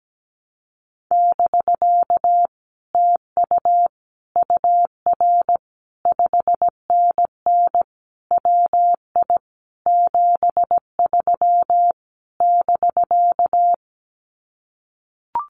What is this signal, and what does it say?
Morse code, heard as a keyed tone, about 17 wpm, 700 Hz. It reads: <BK> TU UR 5NN WI 73 <BK>
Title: Common QSO Phrases Encore 17wpm